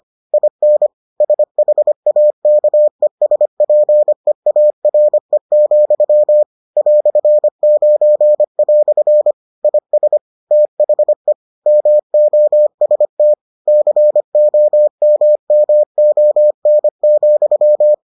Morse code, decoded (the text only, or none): IN SHAKESPEARE, "9" IS THE MOST COMMON,